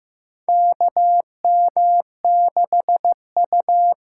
Morse code, heard as a keyed tone, about 15 wpm, 700 Hz. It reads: KM6U
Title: USA Call Signs - 6 Characters - 8x Character Spacing - 15wpm